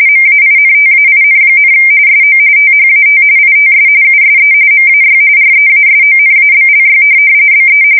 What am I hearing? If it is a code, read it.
RTTY, about 45.45 baud. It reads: MAINTAIN, SO THAT ITS USE MAY REASONABLY BE
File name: fsk170 / printing-telegraph